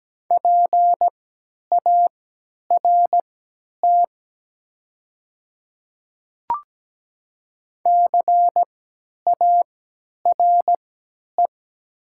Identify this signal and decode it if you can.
Morse code, heard as a keyed tone, about 17 wpm, spaced as for 10 wpm, 700 Hz. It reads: PART CARE
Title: 3x Character Spacing - Top 500 Words - 17wpm